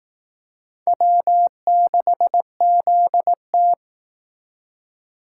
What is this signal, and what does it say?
Morse code, heard as a keyed tone, about 18 wpm, 700 Hz. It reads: W6ZT
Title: Speed Racing - USA 1x2 Call Signs 15wpm